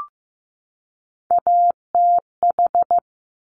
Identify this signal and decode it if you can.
Morse code, heard as a keyed tone, about 15 wpm, 700 Hz. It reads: ATH
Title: Top Three Letter Combinations in Words 15wpm